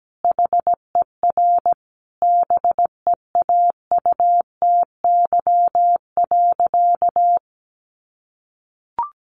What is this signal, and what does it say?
Morse code, heard as a keyed tone, about 17 wpm, 700 Hz. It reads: HER BEAUTY.
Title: Sentences from Top 1300 Words 17wpm